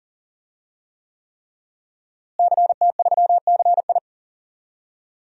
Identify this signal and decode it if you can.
Morse code, heard as a keyed tone, about 40 wpm, 700 Hz. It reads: CT3CI
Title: DXCC-100 Call Signs - 5 Characters - Repeated 3x - 40wpm